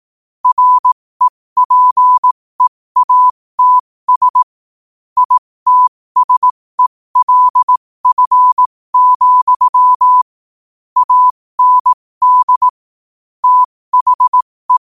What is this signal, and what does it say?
Morse code, heard as a keyed tone, about 18 wpm, spaced as for 15 wpm, 1000 Hz. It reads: REPEATS ITSELF, AND THE
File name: qotdfiles / 15wpm